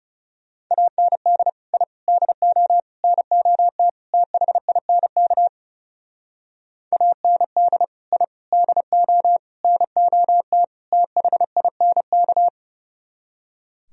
Morse code, heard as a keyed tone, about 36 wpm, 700 Hz. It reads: AND I DO NOT THINK AND I DO NOT THINK
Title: Speed-Racing - Sets of 5 Words 30wpm